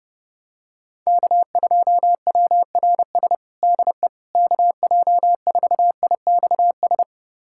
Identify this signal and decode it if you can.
Morse code, heard as a keyed tone, about 30 wpm, 700 Hz. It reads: K2WRS DE KJ4IXS
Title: USA Callsign DE Callsign 30wpm